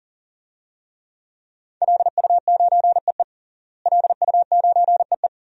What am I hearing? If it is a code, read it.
Morse code, heard as a keyed tone, about 40 wpm, 700 Hz. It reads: LU9EE LU9EE
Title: DXCC-100 Call Signs - 5 Characters - Repeated 2x - 40wpm